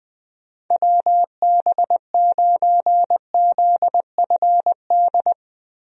Morse code, heard as a keyed tone, about 21 wpm, 700 Hz. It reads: WB9ZFD
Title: USA Call Signs - 6 Characters - Repeated 2x - 20wpm